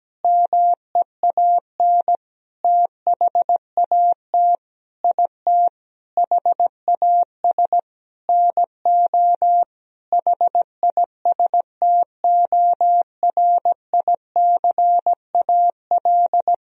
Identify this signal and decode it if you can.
Morse code, heard as a keyed tone, about 17 wpm, 700 Hz. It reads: MEAN THAT IT HAS NO HISTORICAL
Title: Sentences from Top 1000 Words - Review Part 4 - 17wpm